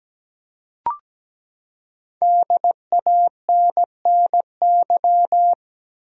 Morse code, heard as a keyed tone, about 17 wpm, 700 Hz. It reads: DANNY
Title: USA Common Names - Max 5 Letters - 17wpm